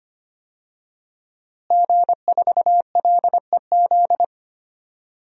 Morse code, heard as a keyed tone, about 25 wpm, 700 Hz. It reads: G4LEZ
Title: DXCC-100 Call Signs - 5 Characters - Repeated 2x - 25wpm